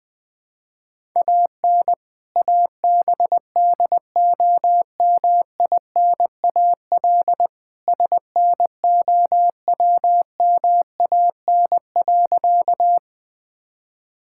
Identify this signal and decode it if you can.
Morse code, heard as a keyed tone, about 20 wpm, 700 Hz. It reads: AN ABDOMINAL SNOWMAN.
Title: All-Age Appropriate Jokes - 20wpm